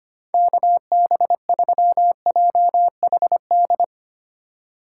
Morse code, heard as a keyed tone, about 25 wpm, 700 Hz. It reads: KB3JHD
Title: USA Call Signs - 6 Characters - 4x Character Spacing - 25wpm